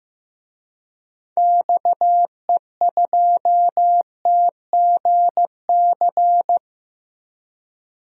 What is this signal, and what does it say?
Morse code, heard as a keyed tone, about 15 wpm, 700 Hz. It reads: XE2TGC
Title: DXCC-100 Call Signs - 6 Characters - Repeated 3x - 15wpm